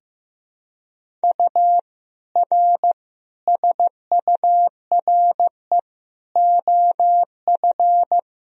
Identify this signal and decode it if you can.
Morse code, heard as a keyed tone, about 15 wpm, 700 Hz. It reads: U R SURE OF
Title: Sentences from Top 300 Words and Abbreviations 15wpm